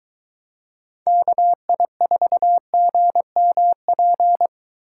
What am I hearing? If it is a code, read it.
Morse code, heard as a keyed tone, about 21 wpm, 700 Hz. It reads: KI4GMP